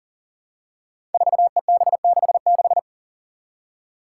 Morse code, heard as a keyed tone, about 40 wpm, 690 Hz. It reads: 4EB66
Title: Five Challenging Letter-Number Combinations 40wpm